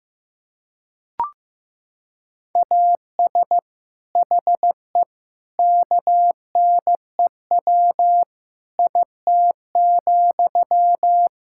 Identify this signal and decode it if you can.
Morse code, heard as a keyed tone, about 15 wpm, 700 Hz. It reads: AS HE KNEW IT,
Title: Sentences from Top 1600 Words 15wpm